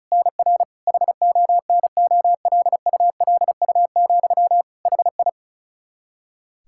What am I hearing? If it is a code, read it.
Morse code, heard as a keyed tone, about 36 wpm, 700 Hz. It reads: NR HONOLULU, HI
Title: Practice QSO - Exchange 1 - Beginning - 35wpm